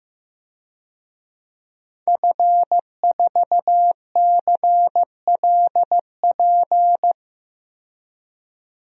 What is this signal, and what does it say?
Morse code, heard as a keyed tone, about 15 wpm, 700 Hz. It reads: F4CLP
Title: DXCC-100 Call Signs - 5 Characters - 15wpm